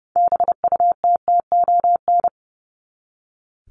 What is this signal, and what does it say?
Morse code, heard as a keyed tone, about 30 wpm, 700 Hz. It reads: BUTTON
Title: Top 1001-1250 Words 30wpm